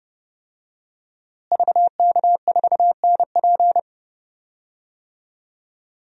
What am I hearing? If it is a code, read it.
Morse code, heard as a keyed tone, about 30 wpm, 700 Hz. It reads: VK4NP
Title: DXCC-100 Call Signs - 5 Characters - Repeated 3x - 30wpm